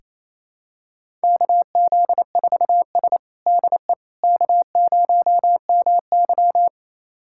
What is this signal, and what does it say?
Morse code, heard as a keyed tone, about 28 wpm, 700 Hz. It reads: KZ4S DE K0MY
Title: USA Callsign DE Callsign - 4 Characters - 28wpm